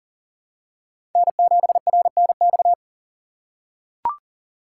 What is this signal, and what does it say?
Morse code, heard as a keyed tone, about 40 wpm, 700 Hz. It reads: N7RNX
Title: USA Call Signs - 5 Characters - 3x Character Spacing - 40wpm